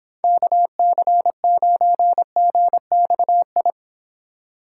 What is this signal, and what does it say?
Morse code, heard as a keyed tone, about 31 wpm, 700 Hz. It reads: KC9GXI